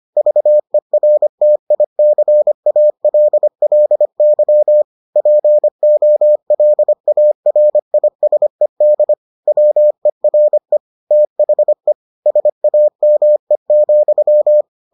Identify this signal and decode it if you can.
Morse code, heard as a keyed tone, about 25 wpm, 600 Hz. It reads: VERTICALLY POLARISED WERE THE SAME,